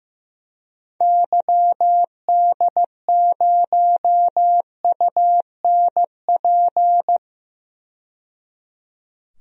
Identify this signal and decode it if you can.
Morse code, heard as a keyed tone, about 15 wpm, 700 Hz. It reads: YD0UNP
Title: DXCC-100 Call Signs - 6 Characters - 15wpm